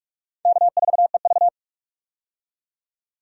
Morse code, heard as a keyed tone, about 46 wpm, 700 Hz. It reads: K4EV
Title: USA Call Signs - 4 Characters - 45wpm